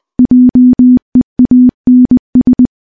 Morse code, heard as a keyed tone, about 20 wpm, 250 Hz. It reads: JEANS
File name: morse-training-20WPM